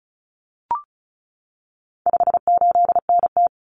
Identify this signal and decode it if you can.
Morse code, heard as a keyed tone, about 34 wpm, 700 Hz. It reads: 58NT
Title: Common Cut Numbers - 4-Digit 35wpm